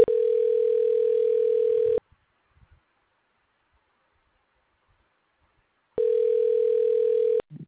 Ringback tone (North America), 440 + 480 Hz.
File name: recordings / REce a27bac1e22cd9534c58c8ef